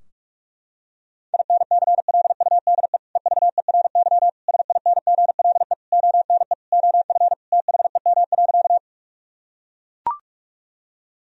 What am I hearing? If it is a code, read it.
Morse code, heard as a keyed tone, about 41 wpm, 700 Hz. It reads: INCLUDE EVERY SINGLE ONE OF THEM.
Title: Sentences from Top 1000 Words - Review Part 4 - 45wpm